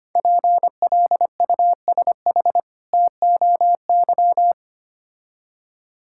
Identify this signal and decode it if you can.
Morse code, heard as a keyed tone, about 25 wpm, 700 Hz. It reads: PLUSH TOY
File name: practice / 25wpm / Difficult Words with Hints - 25wpm